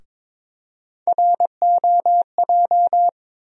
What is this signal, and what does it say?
Morse code, heard as a keyed tone, about 22 wpm, 700 Hz. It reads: ROJ
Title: USA Call Sign Postfixes - 3 Letters 22wpm